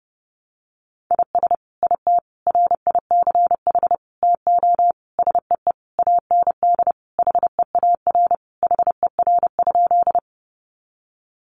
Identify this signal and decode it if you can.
Morse code, heard as a keyed tone, about 30 wpm, 700 Hz. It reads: IS IT RICH TO SEE AND HEAR HER?
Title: Sentences from Top 1100 Words 30wpm